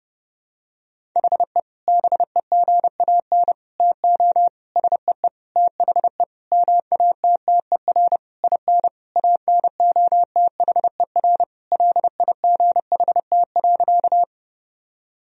Morse code, heard as a keyed tone, about 30 wpm, 700 Hz. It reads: HE BEGAN TO SEE THE MATTER IN ANOTHER LIGHT.
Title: Sentences from Top 1000 Words - Review Part 1 - 30wpm